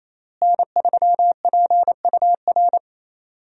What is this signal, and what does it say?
Morse code, heard as a keyed tone, about 28 wpm, 700 Hz. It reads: N3PUR
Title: USA Call Signs - 5 Characters - Repeated 3x - 28wpm